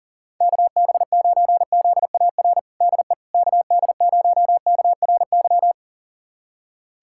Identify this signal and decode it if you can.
Morse code, heard as a keyed tone, about 37 wpm, 690 Hz. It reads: KB9ZAR DE KD0KRY
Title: USA Callsign DE Callsign - 6 Characters - 40wpm